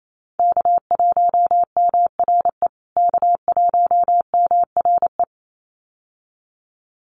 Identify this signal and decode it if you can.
Morse code, heard as a keyed tone, about 28 wpm, 700 Hz. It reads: K1MRE K1MRE
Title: USA Call Signs Repeated 2x 28wpm